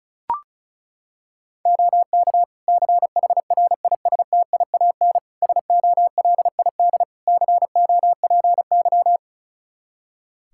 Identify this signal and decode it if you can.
Morse code, heard as a keyed tone, about 36 wpm, 700 Hz. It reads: OK CHRISTIAN SOLID COPY
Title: Practice QSO - Exchange 3 - More Info - 35wpm